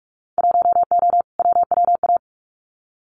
Morse code, heard as a keyed tone, about 46 wpm, 700 Hz. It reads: 1O WWA
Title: Winter Field Day Exchanges 45wpm